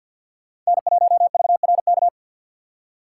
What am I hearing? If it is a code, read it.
Morse code, heard as a keyed tone, about 48 wpm, 700 Hz. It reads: N1VRK
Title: USA Call Signs - 5 Characters - 4x Character Spacing - 50wpm